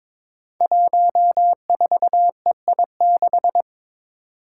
Morse code, heard as a keyed tone, about 22 wpm, 700 Hz. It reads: 14EI6
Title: Five Challenging Letter-Number Combinations 22wpm